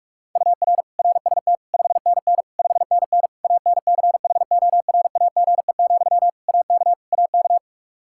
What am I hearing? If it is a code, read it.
Morse code, heard as a keyed tone, about 43 wpm, 700 Hz. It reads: UR RST 5NN 5NN ANCHORAGE, AK AK
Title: Practice QSO - Exchange 1 - Beginning - 45wpm